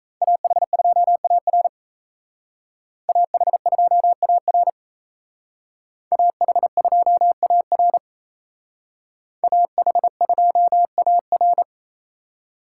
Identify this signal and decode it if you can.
Morse code, heard as a keyed tone, about 36 wpm, 700 Hz. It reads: AH2AR AH2AR AH2AR AH2AR